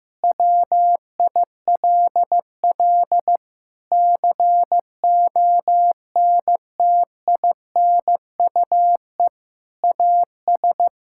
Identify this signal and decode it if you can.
Morse code, heard as a keyed tone, about 15 wpm, 700 Hz. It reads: WILL CONTINUE AS
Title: Sentences from Top 1000 Words - Review Part 6 - 15wpm